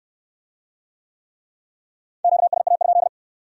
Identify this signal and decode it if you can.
Morse code, heard as a keyed tone, about 67 wpm, 700 Hz. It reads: QSA?